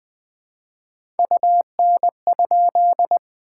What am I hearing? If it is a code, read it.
Morse code, heard as a keyed tone, about 20 wpm, 700 Hz. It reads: UN?
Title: POTA QSO Practice - 20wpm